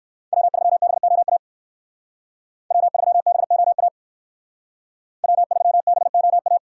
Morse code, heard as a keyed tone, about 53 wpm, 700 Hz. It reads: W3BOA W3BOA W3BOA
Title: Speed Racing - USA 1x3 Call Signs 45wpm